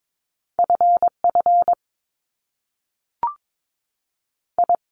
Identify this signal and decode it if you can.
Morse code, heard as a keyed tone, about 22 wpm, spaced as for 3 wpm, 700 Hz. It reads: F FI
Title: Top Two Letter Combinations in Words 22wpm